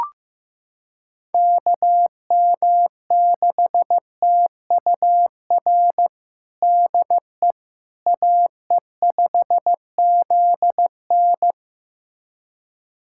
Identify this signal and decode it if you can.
Morse code, heard as a keyed tone, about 16 wpm, 700 Hz. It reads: KM6TUR DE AE5ZN
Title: USA Callsign DE Callsign 15wpm